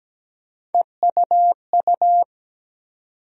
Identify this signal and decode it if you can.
Morse code, heard as a keyed tone, about 17 wpm, 700 Hz. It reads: EUU
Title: USA Call Sign Postfixes - 3 Letters 17wpm